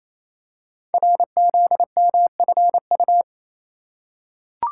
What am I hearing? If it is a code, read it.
Morse code, heard as a keyed tone, about 27 wpm, 700 Hz. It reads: RZMFU
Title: Code Groups - 5 Letter:Numbers - 28wpm